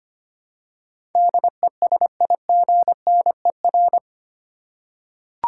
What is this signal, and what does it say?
Morse code, heard as a keyed tone, about 25 wpm, 700 Hz. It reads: DESIGNER
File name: 8 Letter Words - First 3 Letters Given - 25wpm